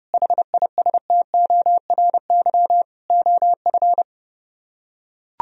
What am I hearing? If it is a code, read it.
Morse code, heard as a keyed tone, about 30 wpm, 700 Hz. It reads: HISTORY OF